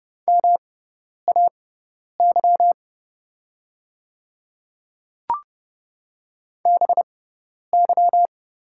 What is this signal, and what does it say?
Morse code, heard as a keyed tone, about 30 wpm, spaced as for 10 wpm, 700 Hz. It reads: MAY BY
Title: 6x Character Spacing - Top 100 Words - 30wpm